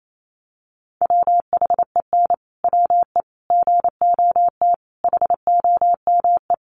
Morse code, heard as a keyed tone, about 28 wpm, 700 Hz. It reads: WHEN WE GOT HOME
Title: Sentences from Top 500 Words 28wpm